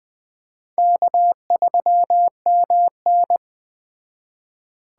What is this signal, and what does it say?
Morse code, heard as a keyed tone, about 20 wpm, 700 Hz. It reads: K3MN